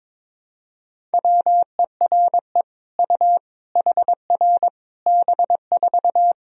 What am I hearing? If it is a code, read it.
Morse code, heard as a keyed tone, about 22 wpm, 700 Hz. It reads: WERE U HR B4
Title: Sentences from Top 200 Words and Abbreviations 22wpm